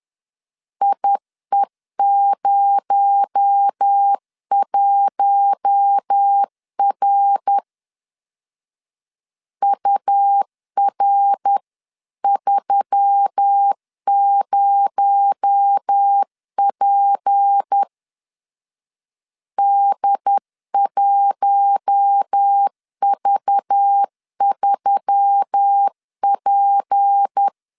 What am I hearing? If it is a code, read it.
Morse code, heard as a keyed tone, about 10 wpm, 800 Hz. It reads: IE01R UR30P D1V3P
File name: LEZIONI / 6 PKX0 / 6 PKX0 10wpm 2707